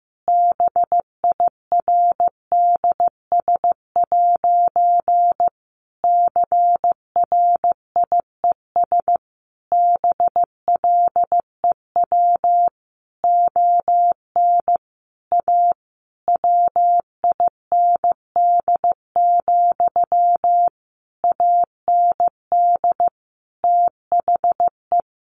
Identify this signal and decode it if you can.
Morse code, heard as a keyed tone, about 15 wpm, 700 Hz. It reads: BIRDS' CRIES BLEW ON A WIND, AND THE
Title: A Sound Of Thunder 15wpm